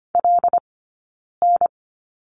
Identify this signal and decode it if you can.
Morse code, heard as a keyed tone, about 25 wpm, spaced as for 9 wpm, 690 Hz. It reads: LN
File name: Lesson 15d - Number 5 - Two Characters 25wpm